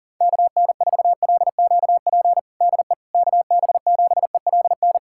Morse code, heard as a keyed tone, about 37 wpm, 700 Hz. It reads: KN4LQP DE KB7ELN
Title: USA Callsign DE Callsign 40wpm